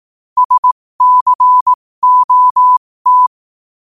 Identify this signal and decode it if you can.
Morse code, heard as a keyed tone, about 18 wpm, spaced as for 15 wpm, 1000 Hz. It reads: SCOT
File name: qotdfiles / 15wpm